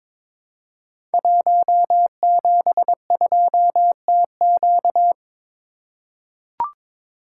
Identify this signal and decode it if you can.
Morse code, heard as a keyed tone, about 23 wpm, 700 Hz. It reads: 172TQ
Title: Code Groups - 5 Letter:Numbers - 22wpm